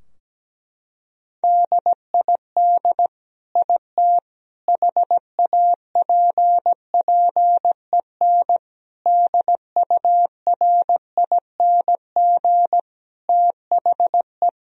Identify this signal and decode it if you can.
Morse code, heard as a keyed tone, about 17 wpm, 700 Hz. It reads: DID IT HAPPEN DURING THE
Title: Sentences from Top 1000 Words - Review Part 2 - 17wpm